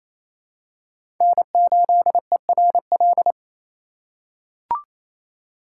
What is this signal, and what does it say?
Morse code, heard as a keyed tone, about 28 wpm, 700 Hz. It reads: N8ERL
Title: USA Call Signs - 5 Characters - 2x Character Spacing - 28wpm